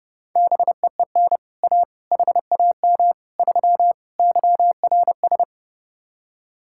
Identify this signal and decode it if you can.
Morse code, heard as a keyed tone, about 33 wpm, 700 Hz. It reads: BEEN A HAM 3 YRS